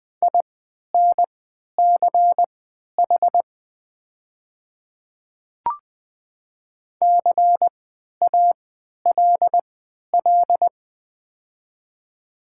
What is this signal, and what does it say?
Morse code, heard as a keyed tone, about 20 wpm, spaced as for 11 wpm, 700 Hz. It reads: INCH CALL
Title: 3x Character Spacing - Top 500 Words - 20wpm